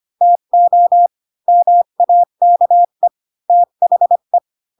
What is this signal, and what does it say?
Morse code, heard as a keyed tone, about 25 wpm, 700 Hz. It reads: TO MAKE THE
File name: Pinocchio / Pinocchio ch1 20 25 ik1hge